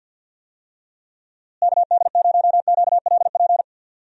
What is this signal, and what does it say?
Morse code, heard as a keyed tone, about 50 wpm, 700 Hz. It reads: KD0QLP
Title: USA Call Signs - 6 Characters - Repeated 3x - 50wpm